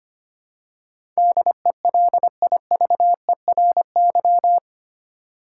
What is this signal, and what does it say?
Morse code, heard as a keyed tone, about 25 wpm, 700 Hz. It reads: DELIVERY
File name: 8 Letter Words - First 3 Letters Given - 25wpm